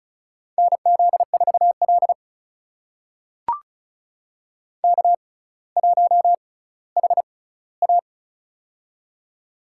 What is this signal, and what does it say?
Morse code, heard as a keyed tone, about 36 wpm, spaced as for 18 wpm, 700 Hz. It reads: NZ4L K1HA
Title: USA Call Signs - 4 Characters - 6x Character Spacing - 35wpm